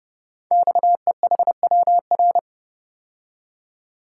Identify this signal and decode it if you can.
Morse code, heard as a keyed tone, about 30 wpm, 700 Hz. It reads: XEHWR
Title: Code Groups - 5 Letters - 30wpm